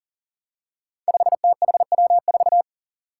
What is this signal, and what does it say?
Morse code, heard as a keyed tone, about 40 wpm, 700 Hz. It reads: 5THW4
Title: Five Challenging Letter-Number Combinations 40wpm